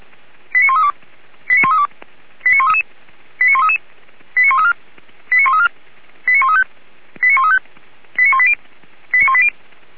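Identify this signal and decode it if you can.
Five-tone selective call (ZVEI), heard as quick runs of five tones, tones of about 70 ms. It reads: 89132 89132 89133 89133 89134 89134 89135 89135 89180 89180
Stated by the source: ZVEI five-tone selective call received on SDR.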